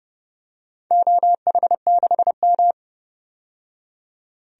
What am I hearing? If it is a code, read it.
Morse code, heard as a keyed tone, about 30 wpm, 700 Hz. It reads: OH6M
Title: DXCC-100 Call Signs - 4 Characters - 30wpm